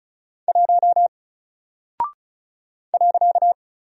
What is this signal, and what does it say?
Morse code, heard as a keyed tone, about 36 wpm, 700 Hz. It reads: 1 .
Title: Single Character (Letter:Number:Punctuation:Prosign) Alt 35wpm